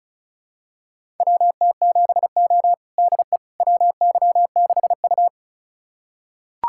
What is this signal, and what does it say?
Morse code, heard as a keyed tone, about 36 wpm, 700 Hz. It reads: WT7O DE WY6U
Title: USA Callsign DE Callsign - 4 Characters - 35wpm